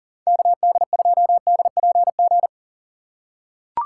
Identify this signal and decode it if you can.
Morse code, heard as a keyed tone, about 40 wpm, 700 Hz. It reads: KD2DPG
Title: USA Call Signs - 8x Character Spacing - 40wpm